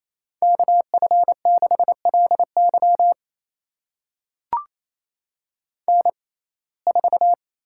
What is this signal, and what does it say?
Morse code, heard as a keyed tone, about 28 wpm, spaced as for 20 wpm, 700 Hz. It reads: KF6LY N4
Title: USA Call Signs - 5 Characters - 6x Character Spacing - 28wpm